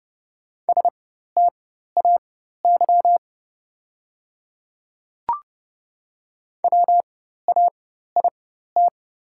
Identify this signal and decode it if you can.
Morse code, heard as a keyed tone, about 30 wpm, spaced as for 14 wpm, 700 Hz. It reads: STAY WAIT
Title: 4x Character Spacing - Top 500 Words - 30wpm